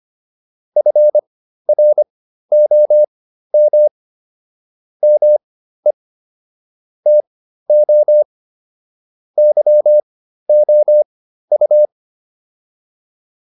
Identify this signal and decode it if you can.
Morse code, heard as a keyed tone, about 25 wpm, spaced as for 13 wpm, 600 Hz. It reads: FROM ME TO YOU